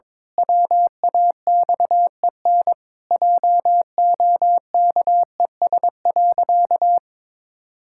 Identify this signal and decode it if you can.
Morse code, heard as a keyed tone, about 22 wpm, 700 Hz. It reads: WAXEN JOKES.
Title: Pangram Challenge v2 - 22wpm